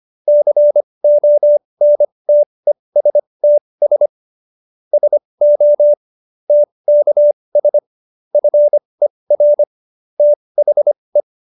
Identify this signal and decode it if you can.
Morse code, heard as a keyed tone, about 25 wpm, spaced as for 20 wpm, 590 Hz. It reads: CONTESTS SO TKS FER THE
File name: QSO316 20